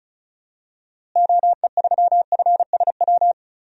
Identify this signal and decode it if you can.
Morse code, heard as a keyed tone, about 34 wpm, 700 Hz. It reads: OE3FSW
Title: DXCC-100 Call Signs - 6 Characters - Repeated 3x - 35wpm